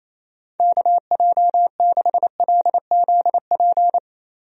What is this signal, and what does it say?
Morse code, heard as a keyed tone, about 28 wpm, 700 Hz. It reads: KJ6LZP
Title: USA Call Signs - 6 Characters - Repeated 2x - 28wpm